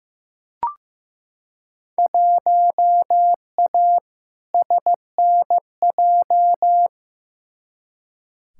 Morse code, heard as a keyed tone, about 15 wpm, 700 Hz. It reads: 1A SNJ
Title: Field Day Exchanges - 15wpm